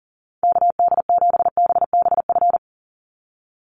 Morse code, heard as a keyed tone, about 40 wpm, 700 Hz. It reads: KD7BBF
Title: USA Call Signs 40wpm